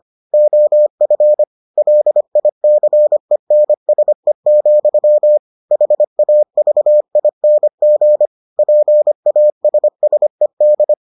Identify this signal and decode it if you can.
Morse code, heard as a keyed tone, about 25 wpm, 600 Hz. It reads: OF LICENSE, HAVING PASSED